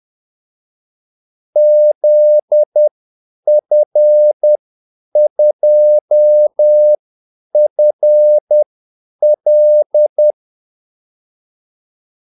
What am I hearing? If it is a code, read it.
Morse code, heard as a keyed tone, about 10 wpm, spaced as for 8 wpm, 600 Hz. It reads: ZF2FL
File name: Callsigns-10-8wpm